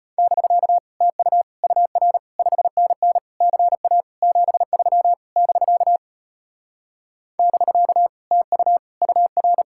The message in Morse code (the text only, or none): <BK> TU UR 5NN CA 73 <BK> <BK> TU UR